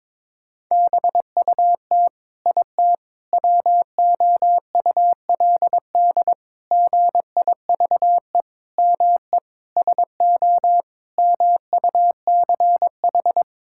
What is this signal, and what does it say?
Morse code, heard as a keyed tone, about 22 wpm, 700 Hz. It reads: BUT IT WOULD GIVE ME SO MUCH
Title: Sentences from Top 1600 Words 22wpm